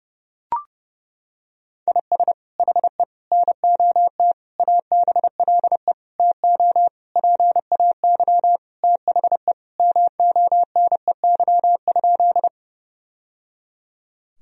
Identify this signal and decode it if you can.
Morse code, heard as a keyed tone, about 30 wpm, 700 Hz. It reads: IS HE NOT ABLE TO PAY THE MONEY?
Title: Sentences from Top 1000 Words - Review Part 2 - 30wpm